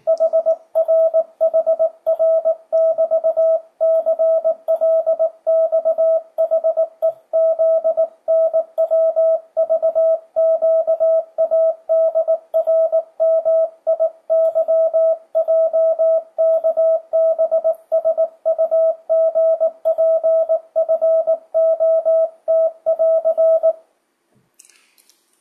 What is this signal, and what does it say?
Morse code, heard as a keyed tone, about 18 wpm, 660 Hz. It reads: HRHR<BT>CLXHEZNWVQADRMIYJKBSUGPFOT<AR>
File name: 2025cw / speedup / 19wpm